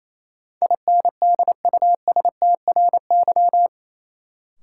Morse code, heard as a keyed tone, about 28 wpm, 700 Hz. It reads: INDUSTRY
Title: Sentences from Top 1000 Words - Review Part 4 - New Word Drill - 28wpm